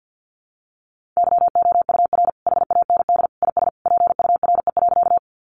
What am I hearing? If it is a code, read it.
Morse code, heard as a keyed tone, about 50 wpm, 700 Hz. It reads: YOUR HAND IS PURE.
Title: Sentences from Top 1500 Words 50wpm